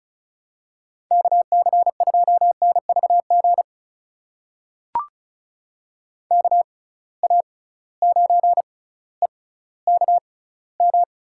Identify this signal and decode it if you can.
Morse code, heard as a keyed tone, about 36 wpm, spaced as for 18 wpm, 690 Hz. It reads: KC2NVG KA9EKM
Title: USA Call Signs - 6 Characters - 6x Character Spacing - 35wpm